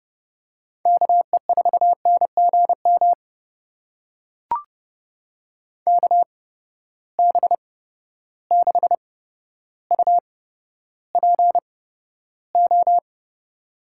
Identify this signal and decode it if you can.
Morse code, heard as a keyed tone, about 32 wpm, spaced as for 13 wpm, 700 Hz. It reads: KE4NGM KB6UPO
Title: USA Call Signs - 6 Characters - 8x Character Spacing - 30wpm